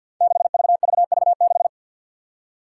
Morse code, heard as a keyed tone, about 52 wpm, 700 Hz. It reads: 6VVV6
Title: Five Challenging Letter-Number Combinations 50wpm